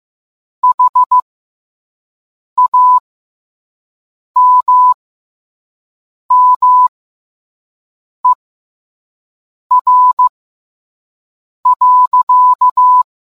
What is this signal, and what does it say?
Morse code, heard as a keyed tone, about 15 wpm, spaced as for 5 wpm, 1000 Hz. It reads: HAMMER.